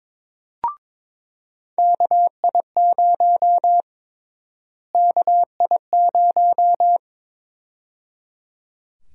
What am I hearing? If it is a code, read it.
Morse code, heard as a keyed tone, about 22 wpm, 700 Hz. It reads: KI0 KI0
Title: USA Call Sign Prefixes - Repeated 2x - 22wpm